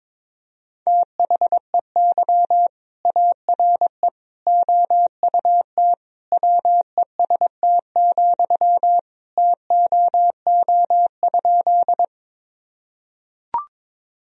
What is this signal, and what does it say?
Morse code, heard as a keyed tone, about 22 wpm, 700 Hz. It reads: THEY ARE OUT WEST, TOO?
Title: Sentences from Top 1000 Words - Review Part 2 - 22wpm